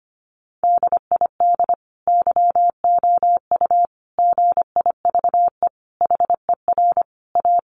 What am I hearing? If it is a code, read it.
Morse code, heard as a keyed tone, about 25 wpm, 700 Hz. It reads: DID YOU GIVE HER A
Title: Sentences from Top 1500 Words 25wpm